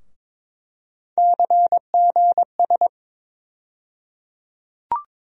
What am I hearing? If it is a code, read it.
Morse code, heard as a keyed tone, about 22 wpm, 700 Hz. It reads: CGS
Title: USA Call Sign Postfixes - 3 Letters 22wpm